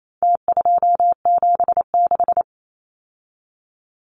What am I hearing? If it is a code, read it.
Morse code, heard as a keyed tone, about 28 wpm, 700 Hz. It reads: T276
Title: Numbers - Contest Style - 4-Digit 28wpm